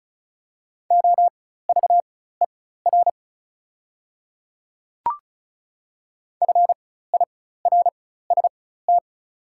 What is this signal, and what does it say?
Morse code, heard as a keyed tone, about 36 wpm, spaced as for 16 wpm, 700 Hz. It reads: OVER FIRST
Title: 4x Character Spacing - Top 100 Words - 35wpm